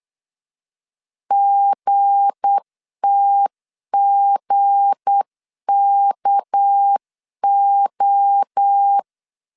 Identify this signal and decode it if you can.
Morse code, heard as a keyed tone, about 8 wpm, 800 Hz. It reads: GTGKO
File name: LEZIONI / 7 GF9Z / 7 G 8wpm 3118